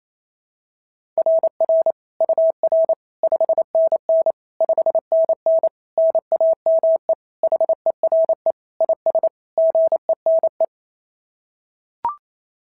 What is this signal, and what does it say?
Morse code, heard as a keyed tone, about 25 wpm, 650 Hz. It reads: RR UR 5NN 5NN NAME HERE IS GENE
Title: SOTA QSO Practice 28wpm